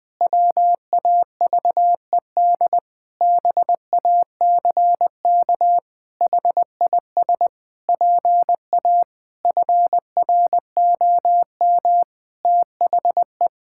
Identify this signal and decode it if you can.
Morse code, heard as a keyed tone, about 20 wpm, 700 Hz. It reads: WAVED BACK HIS PA FROM THE
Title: Pangram Challenge v2 - 20wpm